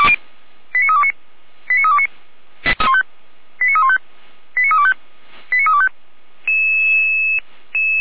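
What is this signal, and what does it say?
Five-tone selective call (ZVEI), heard as quick runs of five tones, tones of about 70 ms. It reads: 89329 89329 89315 89325 89325
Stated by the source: ZVEI five-tone call received by SDR.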